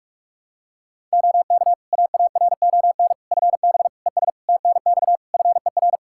Morse code, heard as a keyed tone, about 46 wpm, 700 Hz. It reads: OK AARON FB ES TNX FER
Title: K1USN SST QSO Practice 45wpm